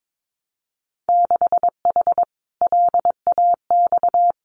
Morse code, heard as a keyed tone, about 22 wpm, 700 Hz. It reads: 6H LAX
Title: Winter Field Day Exchanges Encore 22wpm